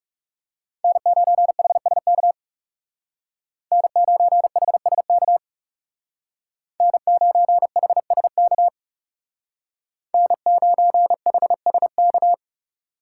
Morse code, heard as a keyed tone, about 37 wpm, 700 Hz. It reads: N9HSK N9HSK N9HSK N9HSK
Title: Speed Racing - USA 1x3 Call Signs 30wpm